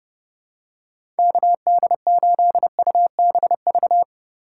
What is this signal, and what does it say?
Morse code, heard as a keyed tone, about 32 wpm, 700 Hz. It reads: KD8UBV
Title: USA Call Signs - 6 Characters - 4x Character Spacing - 30wpm